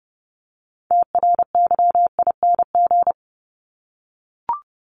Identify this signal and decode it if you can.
Morse code, heard as a keyed tone, about 30 wpm, 700 Hz. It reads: TRYING
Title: ING Suffix - Top 100 Words with Base Words 30wpm